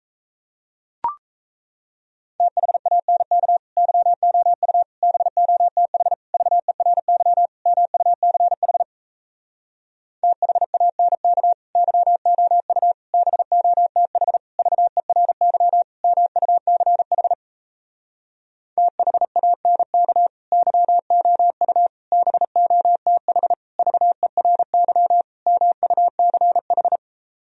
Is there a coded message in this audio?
Yes: Morse code from 2.4 to 27.0 s, heard as a keyed tone, about 38 wpm, 700 Hz. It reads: THANK YOU BOTH VERY MUCH THANK YOU BOTH VERY MUCH THANK YOU BOTH VERY MUCH